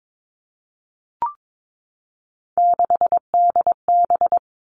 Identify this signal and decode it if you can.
Morse code, heard as a keyed tone, about 22 wpm, 700 Hz. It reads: 6DB
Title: Three Challenging Letter-Number Combinations 22wpm